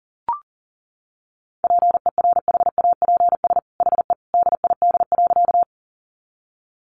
Morse code, heard as a keyed tone, about 40 wpm, 700 Hz. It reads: PERHAPS HE DID.
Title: Sentences from Top 500 Words 40wpm